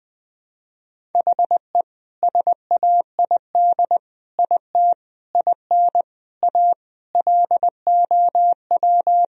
Morse code, heard as a keyed tone, about 20 wpm, 700 Hz. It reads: HE SAID IT IN A LOW
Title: Sentences from Top 1000 Words - Review Part 1 - 20wpm